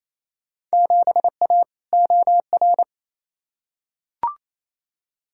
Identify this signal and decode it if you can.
Morse code, heard as a keyed tone, about 28 wpm, 700 Hz. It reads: 7A OR
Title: Field Day Exchanges Encore - 28wpm